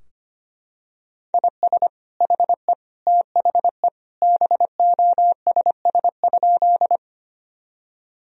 Morse code, heard as a keyed tone, about 25 wpm, 700 Hz. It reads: IS HE THE BOSS?
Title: Sentences from Top 1000 Words - Review Part 4 - 25wpm